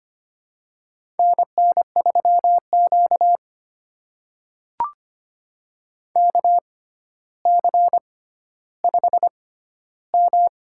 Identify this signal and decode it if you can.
Morse code, heard as a keyed tone, about 25 wpm, spaced as for 13 wpm, 700 Hz. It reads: NN3Q KC5M
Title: USA Call Signs - 4 Characters - 6x Character Spacing - 25wpm